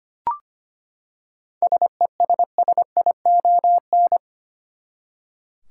Morse code, heard as a keyed tone, about 25 wpm, 700 Hz. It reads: SESSION
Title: Seven Letter Words - 25wpm